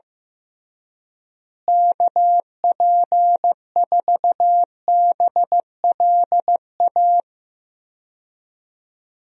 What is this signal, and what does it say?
Morse code, heard as a keyed tone, about 16 wpm, 700 Hz. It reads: KP4BLA
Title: DXCC-100 Call Signs - 6 Characters - Repeated 3x - 15wpm